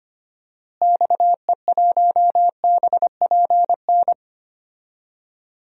Morse code, heard as a keyed tone, about 25 wpm, 700 Hz. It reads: XE1BPN
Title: DXCC-100 Call Signs - 6 Characters - Repeated 2x - 25wpm